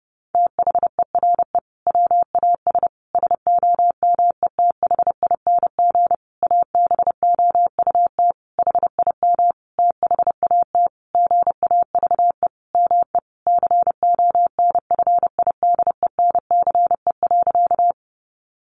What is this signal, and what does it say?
Morse code, heard as a keyed tone, about 28 wpm, 700 Hz. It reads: THERE WAS SOMETHING ABOUT HIM THAT GAVE ME CONFIDENCE.
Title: Sentences from Top 1500 Words 30wpm